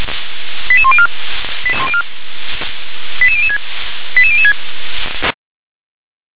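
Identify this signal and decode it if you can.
Five-tone selective call (ZVEI), heard as quick runs of five tones, tones of about 70 ms. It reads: 88104 88006 88006
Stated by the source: ZVEI five-tone selective call received with an SDR.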